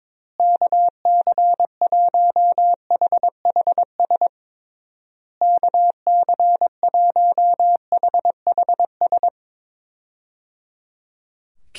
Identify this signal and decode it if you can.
Morse code, heard as a keyed tone, about 21 wpm, 700 Hz. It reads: KC1HHS KC1HHS
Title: USA Call Signs - 6 Characters - Repeated 3x - 22wpm